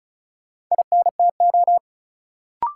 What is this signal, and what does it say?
Morse code, heard as a keyed tone, about 36 wpm, 700 Hz. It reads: INTO
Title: Sentences from Top 600 Words - New Word Drill - 35wpm